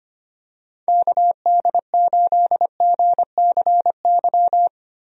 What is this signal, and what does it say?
Morse code, heard as a keyed tone, about 25 wpm, 700 Hz. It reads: KD8GCY
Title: USA Call Signs - 6 Characters - Repeated 3x - 25wpm